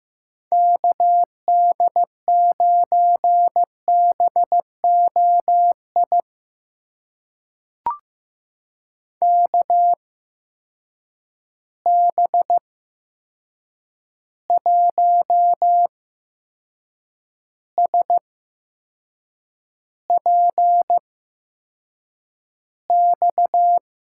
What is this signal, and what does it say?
Morse code, heard as a keyed tone, about 15 wpm, spaced as for 6 wpm, 700 Hz. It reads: KD9BOI KB1SPX
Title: USA Call Signs - 6 Characters - 8x Character Spacing - 15wpm